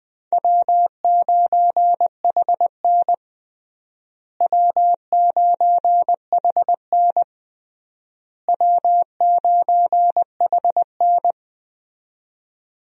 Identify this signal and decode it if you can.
Morse code, heard as a keyed tone, about 20 wpm, 690 Hz. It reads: W9HN W9HN W9HN
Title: USA Call Signs - 4 Characters - Repeated 3x - 20wpm